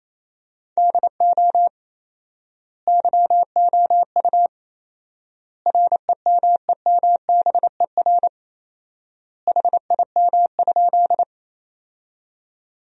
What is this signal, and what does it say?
Morse code, heard as a keyed tone, about 28 wpm, 700 Hz. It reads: DO YOU REMEMBER HIM?
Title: 4x Word-Spacing - Sentences from Top 400 Words - 28wpm